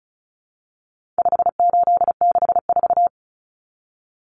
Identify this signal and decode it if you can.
Morse code, heard as a keyed tone, about 34 wpm, 700 Hz. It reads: 5864
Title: Numbers - 4-Digit 35wpm